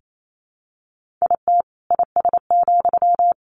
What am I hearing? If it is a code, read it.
Morse code, heard as a keyed tone, about 28 wpm, 700 Hz. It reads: IT IS,
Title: Sentences from Top 1400 Words 28wpm